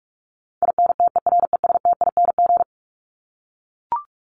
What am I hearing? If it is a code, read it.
Morse code, heard as a keyed tone, about 44 wpm, 700 Hz. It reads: INTERESTING
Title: ING Suffix - Top 100 Words with Base Words 45wpm